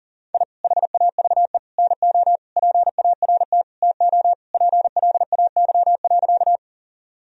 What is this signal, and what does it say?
Morse code, heard as a keyed tone, about 40 wpm, 700 Hz. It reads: I HAVE NO PART TO PLAY.
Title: Sentences from Top 1000 Words - Review Part 1 - 40wpm